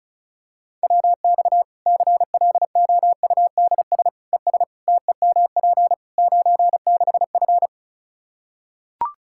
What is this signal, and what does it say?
Morse code, heard as a keyed tone, about 34 wpm, 700 Hz. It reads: WX CLOUDS ES TEMP 96F
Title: Practice QSO - Exchange 2 - Info - 35wpm